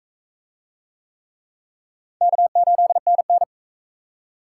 Morse code, heard as a keyed tone, about 39 wpm, 700 Hz. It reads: K8NN